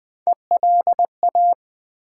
Morse code, heard as a keyed tone, about 20 wpm, 700 Hz. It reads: ELA
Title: Build A Word - 3 Letters At A Time - 9-Letter Words - 20wpm